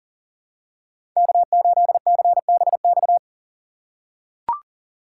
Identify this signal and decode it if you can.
Morse code, heard as a keyed tone, about 40 wpm, 700 Hz. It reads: K8CBX
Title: USA Call Signs - 5 Characters - 2x Character Spacing - 40wpm